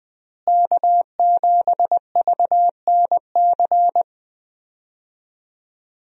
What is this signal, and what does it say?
Morse code, heard as a keyed tone, about 20 wpm, 700 Hz. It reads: K7VNC
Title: USA Call Signs - 5 Characters - Repeated 3x - 20wpm